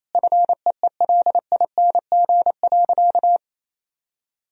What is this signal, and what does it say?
Morse code, heard as a keyed tone, about 28 wpm, 700 Hz. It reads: FEELING.
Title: Sentences from Top 1000 Words - Review Part 3 - 28wpm